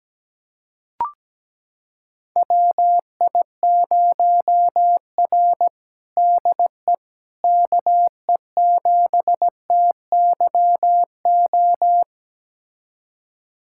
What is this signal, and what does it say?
Morse code, heard as a keyed tone, about 17 wpm, 700 Hz. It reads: WI0R DE KE7TYO
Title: USA Callsign DE Callsign 17wpm